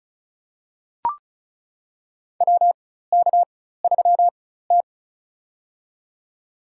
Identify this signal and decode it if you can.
Morse code, heard as a keyed tone, about 36 wpm, spaced as for 16 wpm, 700 Hz. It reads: WK3T
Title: USA Call Signs - 4 Characters - 4x Character Spacing - 35wpm